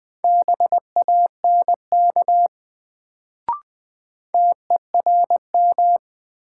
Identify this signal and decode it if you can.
Morse code, heard as a keyed tone, about 20 wpm, 700 Hz. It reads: BANK TERM
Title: Sentences from Top 900 Words - New Word Drill - 20wpm